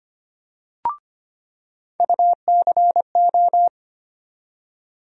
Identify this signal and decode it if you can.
Morse code, heard as a keyed tone, about 25 wpm, 700 Hz. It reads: UCO
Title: USA Call Sign Postfixes - 3 Letters 25wpm